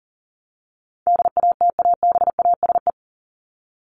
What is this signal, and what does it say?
Morse code, heard as a keyed tone, about 40 wpm, 700 Hz. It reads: DATABASE
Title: Top 1751-2000 Words 40wpm